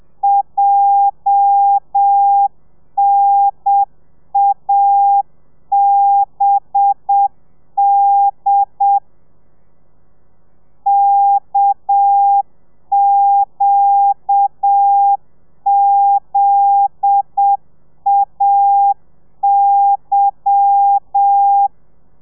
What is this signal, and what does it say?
Morse code, heard as a keyed tone, about 7 wpm, 800 Hz. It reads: JNABD KQZAY
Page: Lezione 1: audio a 30 CPM